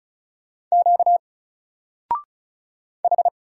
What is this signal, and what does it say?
Morse code, heard as a keyed tone, about 36 wpm, 700 Hz. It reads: Q H
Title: Single Letter-Number 35wpm